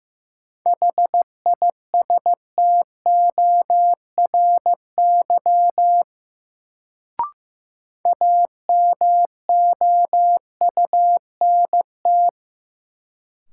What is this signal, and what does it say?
Morse code, heard as a keyed tone, about 15 wpm, 700 Hz. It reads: HISTORY AMOUNT
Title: Sets of 3 Words - New Word Drill - 15wpm